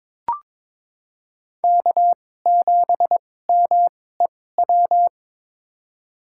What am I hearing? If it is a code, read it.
Morse code, heard as a keyed tone, about 22 wpm, spaced as for 16 wpm, 700 Hz. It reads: K7MEW
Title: USA Call Signs - 2x Character Spacing - 22wpm